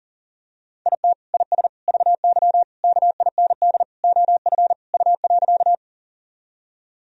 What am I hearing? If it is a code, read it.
Morse code, heard as a keyed tone, about 40 wpm, 700 Hz. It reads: IT IS VY KIND OF U.
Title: Sentences from Top 200 Words and Abbreviations 40wpm